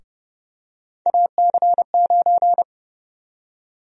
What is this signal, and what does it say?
Morse code, heard as a keyed tone, about 31 wpm, 700 Hz. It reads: AC9
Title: USA Call Sign Prefixes - 30wpm